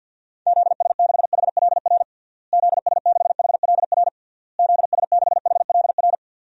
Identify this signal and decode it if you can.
Morse code, heard as a keyed tone, about 50 wpm, 700 Hz. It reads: ZS6HLR ZS6HLR ZS6HLR
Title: DXCC-100 Call Signs - 6 Characters - Repeated 3x - 50wpm